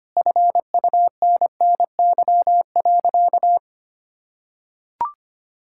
Morse code, heard as a keyed tone, about 25 wpm, 700 Hz. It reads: FUNNY.
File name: practice / 25wpm / Sentences with 4-Word Phrases - 25wpm